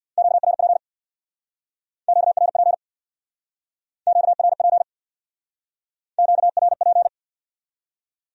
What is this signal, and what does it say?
Morse code, heard as a keyed tone, about 64 wpm, 700 Hz. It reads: QRP QRP QRP QRP